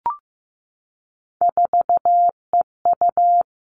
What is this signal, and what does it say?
Morse code, heard as a keyed tone, about 15 wpm, 700 Hz. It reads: 4EU
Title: Three Challenging Letter-Number Combinations 15wpm